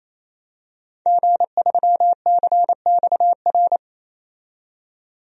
Morse code, heard as a keyed tone, about 27 wpm, 700 Hz. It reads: G3CXR
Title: DXCC-100 Call Signs - 5 Characters - 28wpm